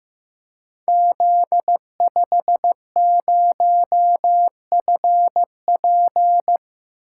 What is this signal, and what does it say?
Morse code, heard as a keyed tone, about 15 wpm, 700 Hz. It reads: Z50FP
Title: Code Groups - 5 Letter:Numbers - 15wpm